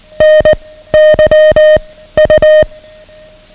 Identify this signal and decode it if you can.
Morse code, heard as a keyed tone, about 19 wpm, 620 Hz. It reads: N Y U
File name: ham radio / CodeQuick / Code Quick 09